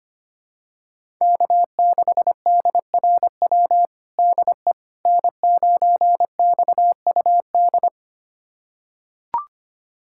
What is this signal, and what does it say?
Morse code, heard as a keyed tone, about 25 wpm, 700 Hz. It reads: K6DRW DE N9XUD
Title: USA Callsign DE Callsign - 5 Characters - 25wpm